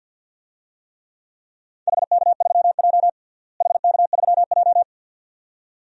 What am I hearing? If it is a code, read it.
Morse code, heard as a keyed tone, about 47 wpm, 700 Hz. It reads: HK3J HK3J
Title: DXCC-100 Call Signs - 4 Characters - Repeated 2x - 50wpm